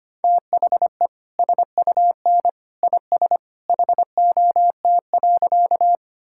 Morse code, heard as a keyed tone, about 25 wpm, 700 Hz. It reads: THE SUN IS HOT.
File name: Easy Sentences with Hints - 25wpm